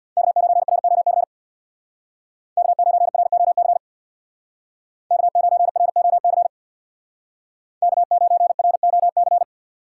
Morse code, heard as a keyed tone, about 65 wpm, 700 Hz. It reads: K9ROC K9ROC K9ROC K9ROC